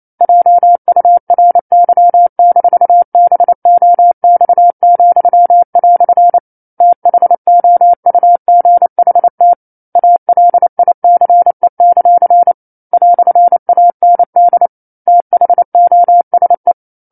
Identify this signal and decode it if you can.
Morse code, heard as a keyed tone, about 28 wpm, 700 Hz. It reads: JURY-BOX," THOUGHT ALICE; "AND THOSE
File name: 09-Chapter-09 28 WPM